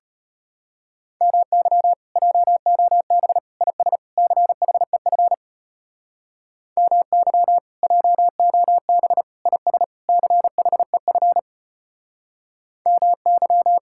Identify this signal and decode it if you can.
Morse code, heard as a keyed tone, about 33 wpm, 700 Hz. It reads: MY JOB IS CHEF MY JOB IS CHEF MY